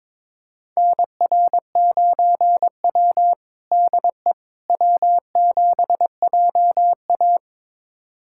Morse code, heard as a keyed tone, about 22 wpm, 700 Hz. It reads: NR9W DE W7JA
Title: USA Callsign DE Callsign - 4 Characters - 22wpm